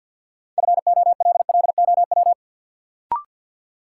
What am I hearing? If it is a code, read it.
Morse code, heard as a keyed tone, about 53 wpm, 700 Hz. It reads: FOLLOW
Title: Sentences from Top 200 Words - New Word Drill - 50wpm